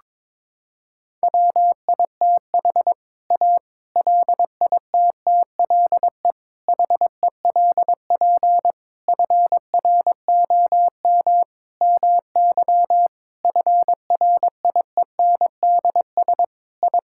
Morse code, heard as a keyed tone, about 22 wpm, 700 Hz. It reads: WITH A LITTLE HELP FROM MY FRIENDS I
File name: Sentences with 7-Word Phrases - 22wpm